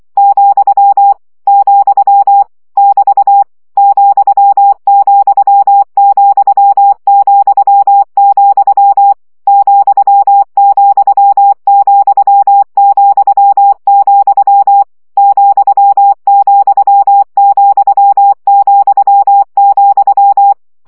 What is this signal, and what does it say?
Morse code, heard as a keyed tone, about 24 wpm, 800 Hz. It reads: , , <BT> ,,,,, ,,,,, ,,,,,
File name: CW gyakorlas vesszo 20 baud